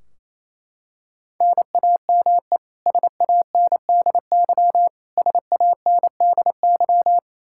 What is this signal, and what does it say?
Morse code, heard as a keyed tone, about 28 wpm, 700 Hz. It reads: NAME SANDY SANDY
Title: Practice QSO - Exchange 1 - Beginning - 28wpm